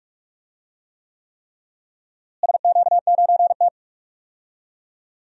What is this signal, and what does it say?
Morse code, heard as a keyed tone, about 48 wpm, 700 Hz. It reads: SQ9T
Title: DXCC-100 Call Signs - 4 Characters - Repeated 3x - 45wpm